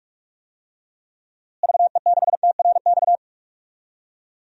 Morse code, heard as a keyed tone, about 41 wpm, 700 Hz. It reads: VE6TRX
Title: DXCC-100 Call Signs - 6 Characters - Repeated 2x - 45wpm